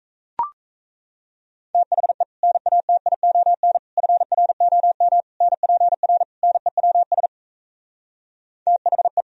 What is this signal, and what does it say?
Morse code, heard as a keyed tone, about 37 wpm, 700 Hz. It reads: THE NATION FROM NPR NEWS THE